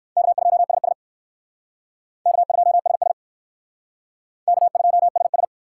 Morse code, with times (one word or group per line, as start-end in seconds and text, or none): 0.2-0.9 K2SS
2.3-3.1 K2SS
4.5-5.4 K2SS